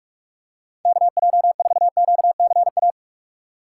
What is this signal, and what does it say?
Morse code, heard as a keyed tone, about 44 wpm, 700 Hz. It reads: KJ4QCA
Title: USA Call Signs - 6 Characters - 6x Character Spacing - 45wpm